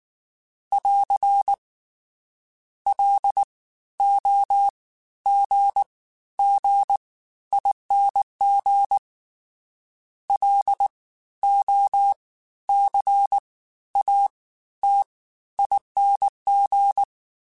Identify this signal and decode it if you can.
Morse code, heard as a keyed tone, about 19 wpm, spaced as for 13 wpm, 800 Hz. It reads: <AR> LOGGING LOCATING
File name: Special INGs-mp3s / Special INGs-mp3s / Fast ing @18